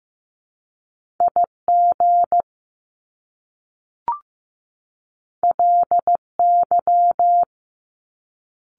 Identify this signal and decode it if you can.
Morse code, heard as a keyed tone, about 15 wpm, 700 Hz. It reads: IG LY
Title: Top Two Letter Combinations in Words 15wpm